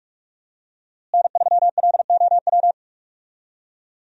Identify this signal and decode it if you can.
Morse code, heard as a keyed tone, about 44 wpm, 700 Hz. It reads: N3LOW
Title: USA Call Signs - 5 Characters - Repeated 3x - 45wpm